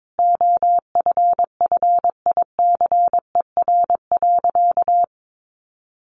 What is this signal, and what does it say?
Morse code, heard as a keyed tone, about 22 wpm, 700 Hz. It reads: OFFICER.
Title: Sentences from Top 1200 Words 22wpm